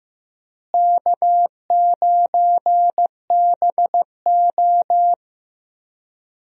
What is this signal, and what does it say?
Morse code, heard as a keyed tone, about 15 wpm, 700 Hz. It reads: K9BO
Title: USA Call Signs - 4 Characters - Repeated 2x - 15wpm